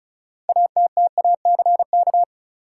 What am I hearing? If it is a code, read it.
Morse code, heard as a keyed tone, about 36 wpm, 700 Hz. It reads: ATTACK
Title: Six Letter Words - 35wpm